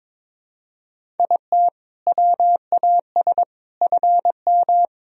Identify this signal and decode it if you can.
Morse code, heard as a keyed tone, about 22 wpm, 700 Hz. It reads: IT WAS FM
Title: Sentences from Top 100 Words and Abbreviations 22wpm